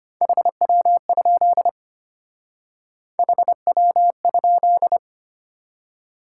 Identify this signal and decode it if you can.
Morse code, heard as a keyed tone, about 31 wpm, 700 Hz. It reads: HW? HW?